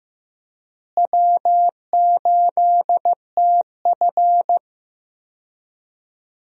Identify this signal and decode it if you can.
Morse code, heard as a keyed tone, about 15 wpm, 700 Hz. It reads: W8TF
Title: USA Call Signs - 4 Characters - Repeated 3x - 15wpm